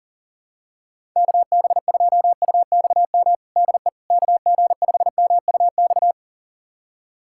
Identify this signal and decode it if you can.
Morse code, heard as a keyed tone, about 40 wpm, 700 Hz. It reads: KB2UXM DE KG5MUX
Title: USA Callsign DE Callsign - 6 Characters - 40wpm